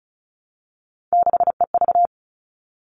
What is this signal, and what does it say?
Morse code, heard as a keyed tone, about 34 wpm, 700 Hz. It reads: 6EV
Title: All Cut Numbers - 3-Digit 35wpm